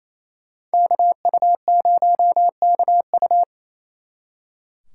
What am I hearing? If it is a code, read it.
Morse code, heard as a keyed tone, about 32 wpm, 700 Hz. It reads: KU0KU
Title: Speed Racing - USA 2x2 Call Signs 28wpm